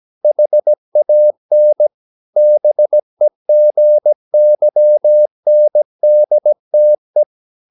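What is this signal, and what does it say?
Morse code, heard as a keyed tone, about 17 wpm, 600 Hz. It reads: HAN BEGYNDTE
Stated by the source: Slow (QRS) Morse practice at 17 wpm, from CW med Gnister.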